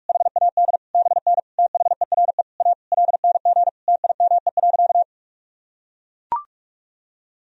Morse code, heard as a keyed tone, about 41 wpm, 700 Hz. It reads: HAD BN THERE A LNG TIME.
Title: Sentences from Top 400 Words and Abbreviations 45wpm